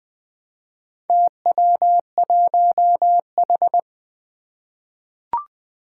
Morse code, heard as a keyed tone, about 20 wpm, 700 Hz. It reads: TW1H
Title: Four Challenging Letter-Number Combinations 20wpm